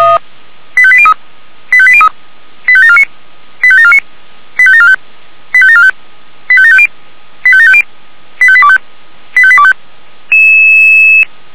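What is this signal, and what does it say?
Five-tone selective call (ZVEI), heard as quick runs of five tones, tones of about 70 ms. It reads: 85803 85803 85749 85749 85745 85745 85750 85750 85825 85825
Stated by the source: ZVEI five-tone selective call received by SDR.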